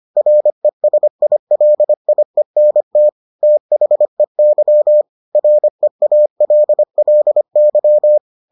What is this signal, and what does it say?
Morse code, heard as a keyed tone, about 25 wpm, 600 Hz. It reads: RESILIENT THEY REALLY